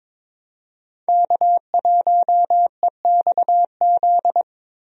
Morse code, heard as a keyed tone, about 22 wpm, 700 Hz. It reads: K1EXZ
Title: USA Call Signs - 5 Characters - Repeated 3x - 22wpm